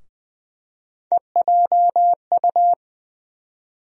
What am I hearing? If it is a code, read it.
Morse code, heard as a keyed tone, about 21 wpm, 700 Hz. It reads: EJU
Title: USA Call Sign Postfixes - 3 Letters 20wpm